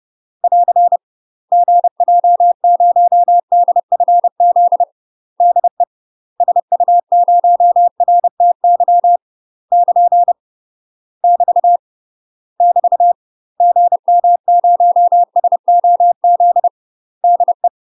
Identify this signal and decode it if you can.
Morse code, heard as a keyed tone, about 30 wpm, 700 Hz. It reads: <AR> GJ0DFZ DE SU0RTY <KN> <BT> <BT> GM0SOZ DE
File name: UK MASTER QSOs TRAINING / 30WPM / UK MASTER QSOs 0046